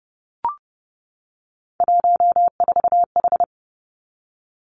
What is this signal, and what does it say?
Morse code, heard as a keyed tone, about 32 wpm, 700 Hz. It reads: 14H
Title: Three Challenging Letter-Number Combinations 30wpm